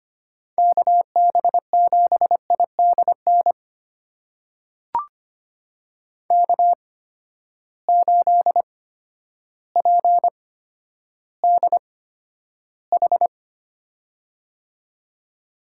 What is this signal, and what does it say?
Morse code, heard as a keyed tone, about 25 wpm, spaced as for 11 wpm, 700 Hz. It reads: KB7IDN K8PDH
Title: USA Call Signs - 8x Character Spacing - 25wpm